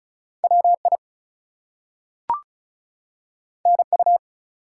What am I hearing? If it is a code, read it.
Morse code, heard as a keyed tone, about 35 wpm, 700 Hz. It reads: WI NU
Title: US States and Canadian Provinces 35wpm